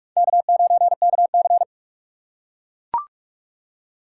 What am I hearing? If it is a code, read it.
Morse code, heard as a keyed tone, about 44 wpm, 700 Hz. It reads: K9KC